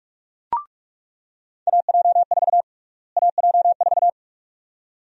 Morse code, heard as a keyed tone, about 42 wpm, 700 Hz. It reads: AJ4 AJ4
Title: USA Call Sign Prefixes - Repeated 2x - 45wpm